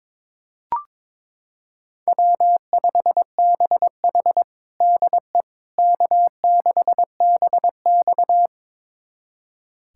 Morse code, heard as a keyed tone, about 22 wpm, 700 Hz. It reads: W5BH DE K6BX
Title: USA Callsign DE Callsign - 4 Characters - 22wpm